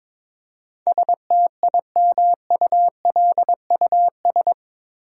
Morse code, heard as a keyed tone, about 22 wpm, 700 Hz. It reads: STIMULUS
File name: practice / 22wpm / Eight Letter Words - 22wpm